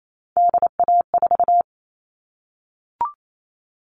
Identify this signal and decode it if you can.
Morse code, heard as a keyed tone, about 28 wpm, 700 Hz. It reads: DA4
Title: All Cut Numbers - 3-Digit 28wpm